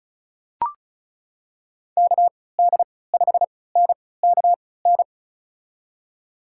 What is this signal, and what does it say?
Morse code, heard as a keyed tone, about 36 wpm, spaced as for 20 wpm, 700 Hz. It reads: KD5NKN
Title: USA Call Signs - 3x Character Spacing - 35wpm